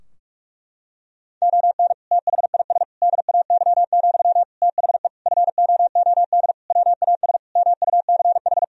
Morse code, heard as a keyed tone, about 46 wpm, 700 Hz. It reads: ON THIS DAY, THE FOOD WAS MUCH
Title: Sentences from Top 1000 Words - Review Part 2 - 45wpm